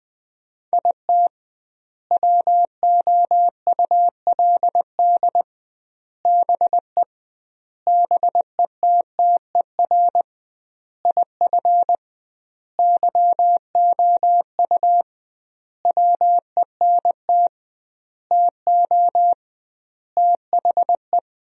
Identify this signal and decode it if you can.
Morse code, heard as a keyed tone, about 21 wpm, 700 Hz. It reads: IT WOULD BE BETTER IF YOU WENT TO THE
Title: 2x Word-Spacing - Sentences from Top 400 Words - 20wpm